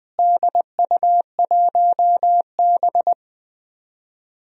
Morse code, heard as a keyed tone, about 20 wpm, 700 Hz. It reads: DU1B
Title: DXCC-100 Call Signs - 4 Characters - Repeated 3x - 20wpm